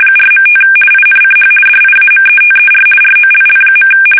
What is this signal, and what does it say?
RTTY, about 50 baud. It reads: 4.5 INCH DIAMETER ROLL.
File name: fsk850-50B / teletype-corp